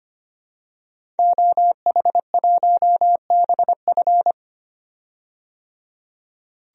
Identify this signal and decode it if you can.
Morse code, heard as a keyed tone, about 25 wpm, 700 Hz. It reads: OH1BF
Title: DXCC-100 Call Signs - 5 Characters - 25wpm